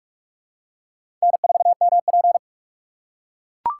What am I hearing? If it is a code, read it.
Morse code, heard as a keyed tone, about 44 wpm, 700 Hz. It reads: N4MP
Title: USA Call Signs - 4 Characters - 3x Character Spacing - 45wpm